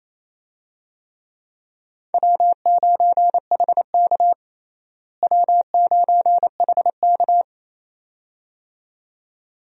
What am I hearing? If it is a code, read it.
Morse code, heard as a keyed tone, about 27 wpm, 700 Hz. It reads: W9HK W9HK
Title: USA Call Signs - 4 Characters - Repeated 2x - 28wpm